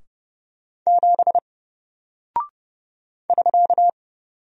Morse code, heard as a keyed tone, about 30 wpm, 700 Hz. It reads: 7 <SK>
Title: Single Character (Letter:Number:Punctuation:Prosign) 30wpm